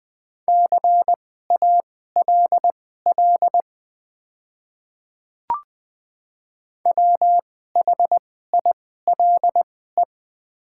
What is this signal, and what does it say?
Morse code, heard as a keyed tone, about 21 wpm, spaced as for 15 wpm, 700 Hz. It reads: CALL WHILE
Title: 2x Character Spacing - Top 300 Words - 20wpm